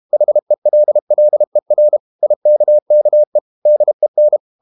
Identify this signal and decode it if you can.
Morse code, heard as a keyed tone, about 32 wpm, 600 Hz. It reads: HELLER IKKE DEN
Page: Ferie 32 wpm